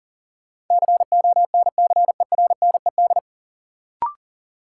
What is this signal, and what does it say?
Morse code, heard as a keyed tone, about 39 wpm, 700 Hz. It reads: CONCERNED
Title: Sentences from Top 1000 Words - Review Part 6 - New Word Drill - 40wpm